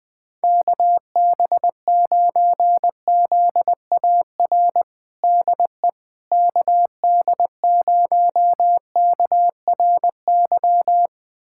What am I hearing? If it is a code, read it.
Morse code, heard as a keyed tone, about 20 wpm, 700 Hz. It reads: KB9ZAR DE KD0KRY
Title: USA Callsign DE Callsign - 6 Characters - 20wpm